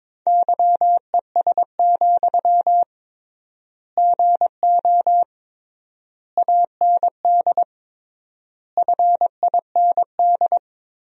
Morse code, heard as a keyed tone, about 22 wpm, 700 Hz. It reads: YES, GO AND FIND
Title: 3x Word-Spacing - Sentences from Top 500 Words - 22wpm